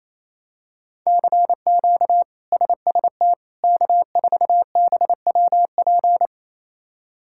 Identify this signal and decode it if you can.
Morse code, heard as a keyed tone, about 27 wpm, 700 Hz. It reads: CQ SST K4BWP
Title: K1USN SST QSO Practice - 28wpm